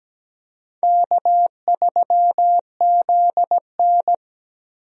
Morse code, heard as a keyed tone, about 17 wpm, 700 Hz. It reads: K3ZN
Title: USA Call Signs - 4 Characters - Repeated 2x - 17wpm